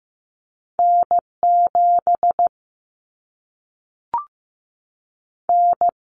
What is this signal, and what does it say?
Morse code, heard as a keyed tone, about 15 wpm, 700 Hz. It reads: N7 N
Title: Common Cut Numbers - 2-Digit 15wpm